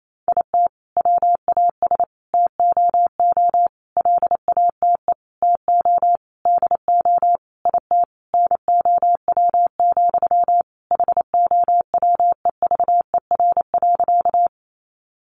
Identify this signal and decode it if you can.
Morse code, heard as a keyed tone, about 28 wpm, 700 Hz. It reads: IT WAS TOO LATE TO DO IT NOW, HOWEVER.
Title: Sentences from Top 600 Words 28wpm